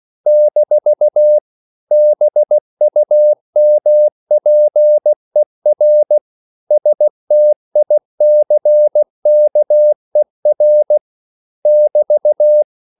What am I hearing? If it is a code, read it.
Morse code, heard as a keyed tone, about 16 wpm, 600 Hz. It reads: - BUMPER STICKER <BT>
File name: CW / Practice / EnglishQuotations0216